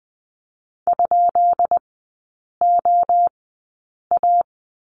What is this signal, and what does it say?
Morse code, heard as a keyed tone, about 20 wpm, spaced as for 8 wpm, 700 Hz. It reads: ?OA
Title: 138 Lesson 23e - Letter f - Three Characters 20wpm